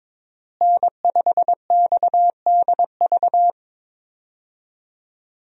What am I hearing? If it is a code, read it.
Morse code, heard as a keyed tone, about 22 wpm, 700 Hz. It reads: N5XDV
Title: USA Call Signs - 5 Characters - Repeated 2x - 22wpm